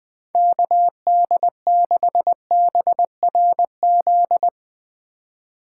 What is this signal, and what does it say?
Morse code, heard as a keyed tone, about 21 wpm, 700 Hz. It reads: KD6BRZ